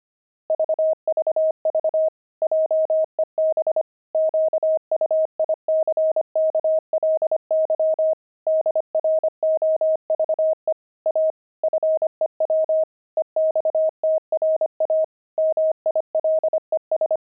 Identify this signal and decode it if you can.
Morse code, heard as a keyed tone, about 25 wpm, 630 Hz. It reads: VVV JEB QUICKLY DROVE A FEW EXTRA MILES
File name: pangram-fs-25-cs-25-pavement